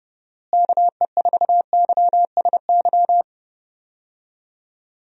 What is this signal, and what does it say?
Morse code, heard as a keyed tone, about 30 wpm, 700 Hz. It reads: KE4YSY
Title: USA Call Signs - 6 Characters - Repeated 2x - 30wpm